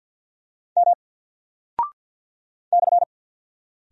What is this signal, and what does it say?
Morse code, heard as a keyed tone, about 48 wpm, 700 Hz. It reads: M /
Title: Single Character (Letter:Number:Punctuation:Prosign) Alt 50wpm